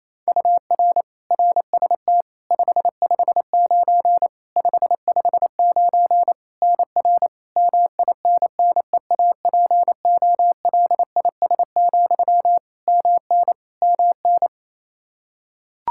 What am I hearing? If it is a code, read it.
Morse code, heard as a keyed tone, about 28 wpm, 700 Hz. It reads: UR RST 559 559 NR MINNEAPOLIS, MN MN
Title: Practice QSO - Exchange 1 - Beginning - 28wpm